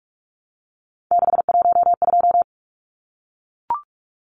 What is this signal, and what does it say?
Morse code, heard as a keyed tone, about 44 wpm, 700 Hz. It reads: 612
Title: Numbers - 3-Digit 45wpm